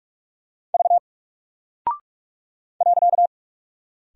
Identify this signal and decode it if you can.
Morse code, heard as a keyed tone, about 44 wpm, 700 Hz. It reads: V .
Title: Single Character (Letter:Number:Punctuation:Prosign) Alt 45wpm